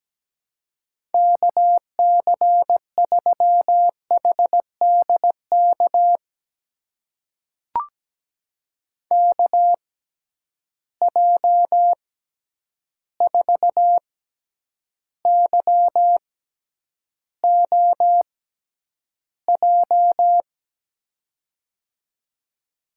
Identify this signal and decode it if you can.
Morse code, heard as a keyed tone, about 17 wpm, spaced as for 9 wpm, 700 Hz. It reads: KC3HDK KJ4YOJ
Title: USA Call Signs - 6 Characters - 6x Character Spacing - 17wpm